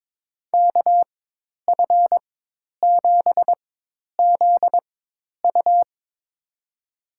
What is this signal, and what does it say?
Morse code, heard as a keyed tone, about 21 wpm, spaced as for 10 wpm, 700 Hz. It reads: KF7ZU
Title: USA Call Signs - 5 Characters - 4x Character Spacing - 22wpm